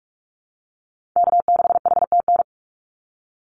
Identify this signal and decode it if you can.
Morse code, heard as a keyed tone, about 44 wpm, 700 Hz. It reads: K6HTN
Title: Call Signs - 45wpm